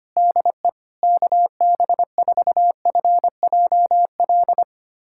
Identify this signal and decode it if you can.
Morse code, heard as a keyed tone, about 25 wpm, 700 Hz. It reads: DE KB4FJL
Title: USA Callsign DE Callsign - 6 Characters - 25wpm